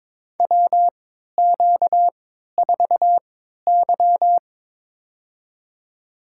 Morse code, heard as a keyed tone, about 22 wpm, spaced as for 12 wpm, 700 Hz. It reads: WQ4Y
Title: USA Call Signs - 4 Characters - 3x Character Spacing - 22wpm